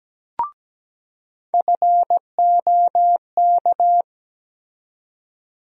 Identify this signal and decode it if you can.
Morse code, heard as a keyed tone, about 17 wpm, 700 Hz. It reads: FOK
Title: USA Call Sign Postfixes - 3 Letters 17wpm